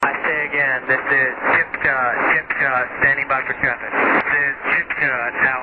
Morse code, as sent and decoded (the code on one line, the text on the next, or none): none